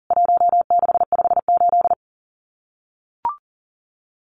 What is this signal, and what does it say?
Morse code, heard as a keyed tone, about 40 wpm, 700 Hz. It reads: 1658
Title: CWT Exchange 40wpm